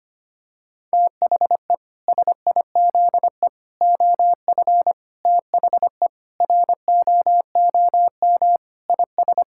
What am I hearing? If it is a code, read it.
Morse code, heard as a keyed tone, about 25 wpm, 700 Hz. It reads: THE SIZE OF THE ROOM IS
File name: practice / 25wpm / Sentences with 3-Word Phrases - 25wpm